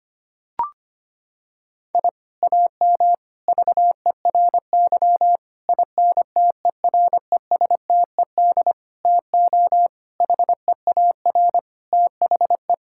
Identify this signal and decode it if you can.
Morse code, heard as a keyed tone, about 25 wpm, 700 Hz. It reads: I AM VERY INTERESTED TO HEAR THE
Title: Sentences from Top 1000 Words - Review Part 6 - 25wpm